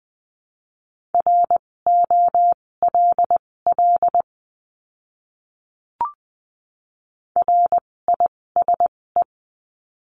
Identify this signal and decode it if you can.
Morse code, heard as a keyed tone, about 20 wpm, spaced as for 16 wpm, 700 Hz. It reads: ROLL RISE
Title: 256 Lesson 42a - Words 12-20wpm